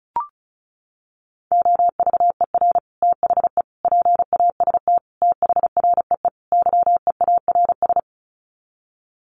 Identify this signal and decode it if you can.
Morse code, heard as a keyed tone, about 34 wpm, 700 Hz. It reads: OVER THE PAST THREE YEARS
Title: Sets of 5 Words Encore2 35wpm